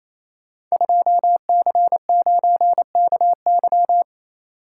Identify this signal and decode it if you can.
Morse code, heard as a keyed tone, about 28 wpm, 700 Hz. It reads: 2C9KY
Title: Code Groups - 5 Letter:Numbers - 28wpm